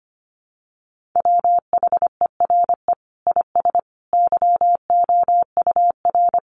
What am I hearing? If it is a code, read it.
Morse code, heard as a keyed tone, about 25 wpm, 700 Hz. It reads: WHERE IS YOUR
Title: Sentences from Top 1200 Words 25wpm